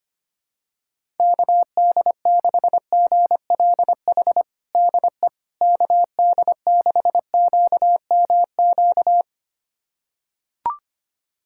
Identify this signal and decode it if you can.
Morse code, heard as a keyed tone, about 25 wpm, 700 Hz. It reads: KD6GLH DE KD6QMQ
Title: USA Callsign DE Callsign - 6 Characters - 25wpm